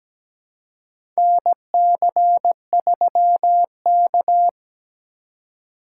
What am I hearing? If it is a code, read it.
Morse code, heard as a keyed tone, about 17 wpm, 700 Hz. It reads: NC3K
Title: USA Call Signs - 4 Characters - 4x Character Spacing - 17wpm